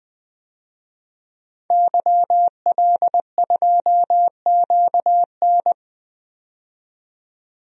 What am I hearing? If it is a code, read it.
Morse code, heard as a keyed tone, about 20 wpm, 700 Hz. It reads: YL2QN
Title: DXCC-100 Call Signs - 5 Characters - 20wpm